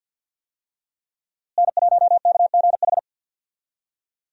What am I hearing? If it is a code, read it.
Morse code, heard as a keyed tone, about 51 wpm, 700 Hz. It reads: N1KGH
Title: USA Call Signs - 5 Characters - Repeated 2x - 50wpm